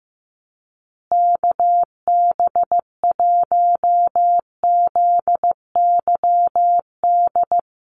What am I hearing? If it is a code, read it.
Morse code, heard as a keyed tone, about 15 wpm, 700 Hz. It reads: KB1ZYD
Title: USA Call Signs Repeated 3x 15wpm